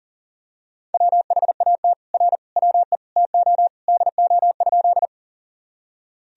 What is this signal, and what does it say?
Morse code, heard as a keyed tone, about 40 wpm, 700 Hz. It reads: WHAT R WE TO DO?
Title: Sentences from Top 100 Words and Abbreviations 40wpm